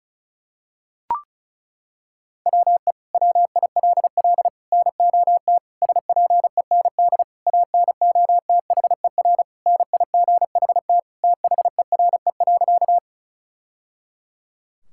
Morse code, heard as a keyed tone, about 34 wpm, 700 Hz. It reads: WE WILL NOT SPEND ANOTHER NIGHT THERE.
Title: Sentences from Top 1000 Words - Review Part 5 - 35wpm